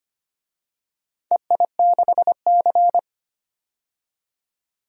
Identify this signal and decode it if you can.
Morse code, heard as a keyed tone, about 25 wpm, 700 Hz. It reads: EI6C
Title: DXCC-100 Call Signs - 4 Characters - 25wpm